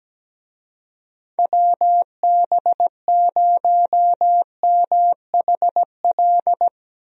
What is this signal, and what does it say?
Morse code, heard as a keyed tone, about 17 wpm, 700 Hz. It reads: WB0MHL
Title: USA Call Signs - 6 Characters - Repeated 2x - 17wpm